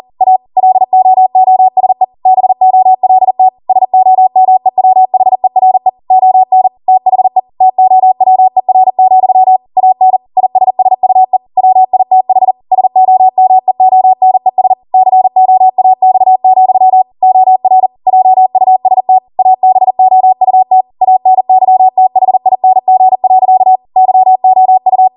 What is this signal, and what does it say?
Morse code, heard as a keyed tone, about 40 wpm, 750 Hz. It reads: A LOOSE BOLT SOMEWHERE ON THE TOWER, AN ISSUE WITH SOMEONES COAX, OR JUST ABOUT ANYTHING. YOU
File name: Morse / 40WPM